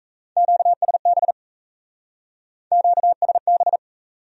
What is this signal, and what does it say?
Morse code, heard as a keyed tone, about 37 wpm, 700 Hz. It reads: QSB QSB